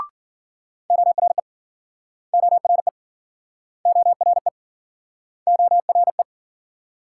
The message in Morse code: ORE ORE ORE ORE

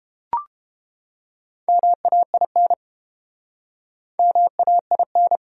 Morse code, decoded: MAIN MAIN

